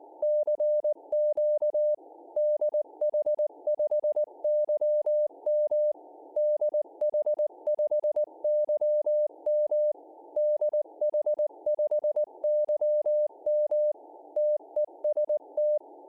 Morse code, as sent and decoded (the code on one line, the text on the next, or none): -.-. --.- / -.. .... ..... -.-- -- / -.. .... ..... -.-- -- / -.. .... ..... -.-- -- / - . ... -
CQ DH5YM DH5YM DH5YM TEST